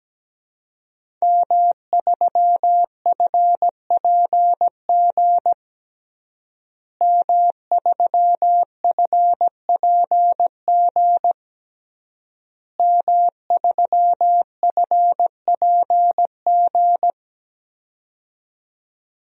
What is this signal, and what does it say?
Morse code, heard as a keyed tone, about 17 wpm, 700 Hz. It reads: M3FPG M3FPG M3FPG
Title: DXCC-100 Call Signs - 5 Characters - Repeated 3x - 17wpm